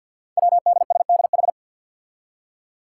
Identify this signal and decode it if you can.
Morse code, heard as a keyed tone, about 52 wpm, 700 Hz. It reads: WDSDH
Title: Five Challenging Letter-Number Combinations 50wpm